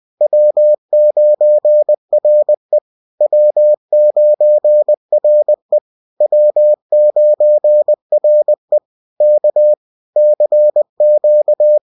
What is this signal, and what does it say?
Morse code, heard as a keyed tone, about 20 wpm, 600 Hz. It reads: W9RE W9RE W9RE K CQ